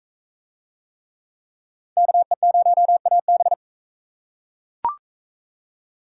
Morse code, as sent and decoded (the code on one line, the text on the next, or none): -.- . ----- .- -...
KE0AB